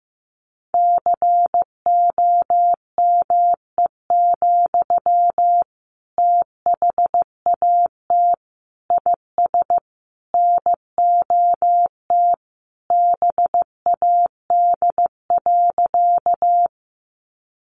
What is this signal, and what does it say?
Morse code, heard as a keyed tone, about 14 wpm, 700 Hz. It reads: COME, THAT IS NOT BAD.
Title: Sentences from Top 700 Words 15wpm